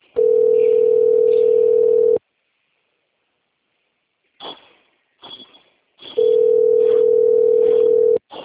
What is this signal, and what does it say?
Ringback tone (North America), 440 + 480 Hz.
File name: rg-1600-unknown- - - .43560